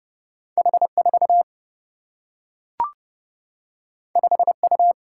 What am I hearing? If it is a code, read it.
Morse code, heard as a keyed tone, about 30 wpm, 700 Hz. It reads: H 4 5 U
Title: Single-Letter Single-Number 30wpm